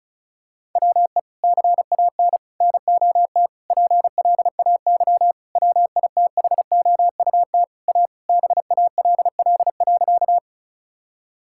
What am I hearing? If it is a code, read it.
Morse code, heard as a keyed tone, about 34 wpm, 700 Hz. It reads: WE CAN NOT PLAY WITHOUT A BALL.
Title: Sentences from Top 1000 Words - Review Part 6 - 35wpm